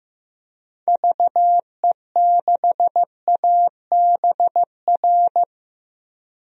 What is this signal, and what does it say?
Morse code, heard as a keyed tone, about 15 wpm, 700 Hz. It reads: VE6ABR
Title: DXCC-100 Call Signs - 6 Characters - Repeated 2x - 15wpm